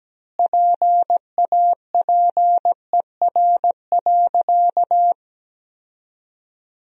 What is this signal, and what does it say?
Morse code, heard as a keyed tone, about 17 wpm, 700 Hz. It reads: PAPER.
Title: 3x Word-Spacing - Sentences from Top 300 Words - 17wpm